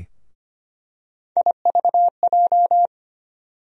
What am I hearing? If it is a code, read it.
Morse code, heard as a keyed tone, about 25 wpm, 700 Hz. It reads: IVJ
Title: USA Call Sign Postfixes - 3 Letters 25wpm